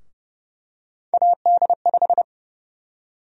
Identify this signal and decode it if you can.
Morse code, heard as a keyed tone, about 30 wpm, 700 Hz. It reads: AD5
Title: USA Call Sign Prefixes - Repeated 2x - 30wpm